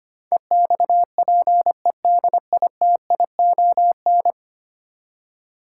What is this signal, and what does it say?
Morse code, heard as a keyed tone, about 25 wpm, 700 Hz. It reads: EXPEDITION
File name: 10 Letter Words - First 3 Letters Given - 25wpm